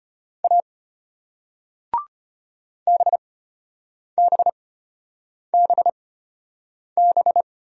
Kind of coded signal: Morse code

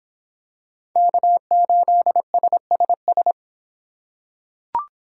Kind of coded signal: Morse code

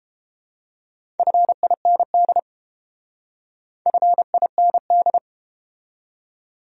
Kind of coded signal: Morse code

